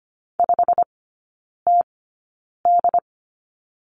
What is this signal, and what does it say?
Morse code, heard as a keyed tone, about 25 wpm, spaced as for 9 wpm, 690 Hz. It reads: 5TD
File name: Lesson 15e - Number 5 - Three Characters 25wpm